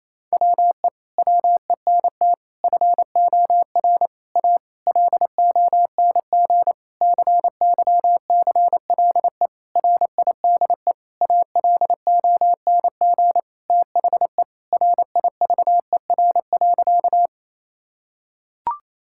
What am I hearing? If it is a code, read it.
Morse code, heard as a keyed tone, about 27 wpm, 700 Hz. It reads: WE WENT FOR A LONG CYCLE RIDE ALONG THE RIVER.
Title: Sentences from Top 1000 Words - Review Part 6 - 28wpm